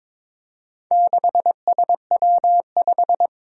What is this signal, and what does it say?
Morse code, heard as a keyed tone, about 22 wpm, 700 Hz. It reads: 6SW5
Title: Four Challenging Letter-Number Combinations 22wpm